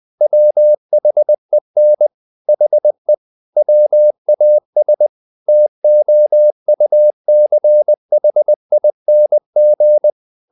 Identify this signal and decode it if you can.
Morse code, heard as a keyed tone, about 20 wpm, 600 Hz. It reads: WHEN HE WAS TOUCHING
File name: the old man and the sea / the old man and the sea part 1 20wpm